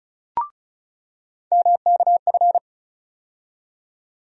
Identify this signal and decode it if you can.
Morse code, heard as a keyed tone, about 34 wpm, 700 Hz. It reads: MKF
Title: USA Call Sign Postfixes - 3 Letters 35wpm